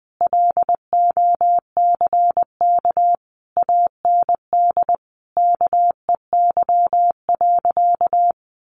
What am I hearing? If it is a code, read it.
Morse code, heard as a keyed tone, about 19 wpm, 700 Hz. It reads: LOCK AND KEY.
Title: Sentences from Top 1400 Words 20wpm